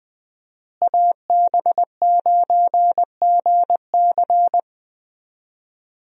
Morse code, heard as a keyed tone, about 20 wpm, 700 Hz. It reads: AB9GC
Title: USA Call Signs - 5 Characters - Repeated 2x - 20wpm